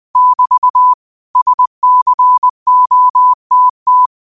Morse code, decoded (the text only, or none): <BT> SCOTT